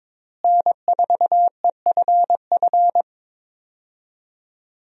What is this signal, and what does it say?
Morse code, heard as a keyed tone, about 22 wpm, 700 Hz. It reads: N4EFF
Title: USA Call Signs - 5 Characters - Repeated 3x - 22wpm